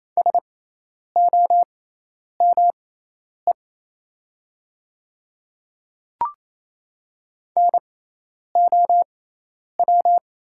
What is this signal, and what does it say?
Morse code, heard as a keyed tone, about 27 wpm, spaced as for 10 wpm, 700 Hz. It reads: SOME NOW
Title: 6x Character Spacing - Top 100 Words - 28wpm